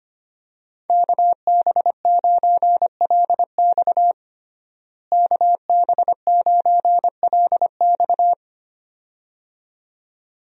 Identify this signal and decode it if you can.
Morse code, heard as a keyed tone, about 25 wpm, 700 Hz. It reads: KB9LX KB9LX
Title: USA Call Signs - 5 Characters - Repeated 2x - 25wpm